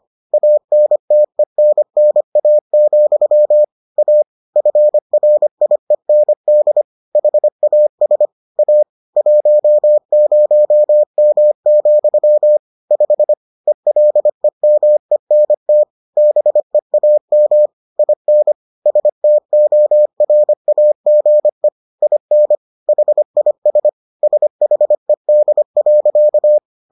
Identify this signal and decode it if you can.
Morse code, heard as a keyed tone, about 25 wpm, 600 Hz. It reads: ANTENNA, A FRIEND HAS A 10M, 5 ELEMENT BEAM IN STORAGE IN HIS SHED.